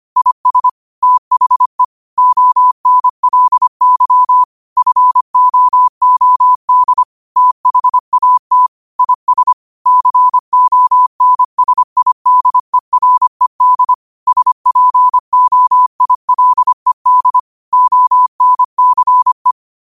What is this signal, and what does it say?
Morse code, heard as a keyed tone, about 25 wpm, 1000 Hz. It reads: IS THE ONLY FOOD THAT IS CONSIDERED SPOILED ONCE
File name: qotdfiles / 25wpm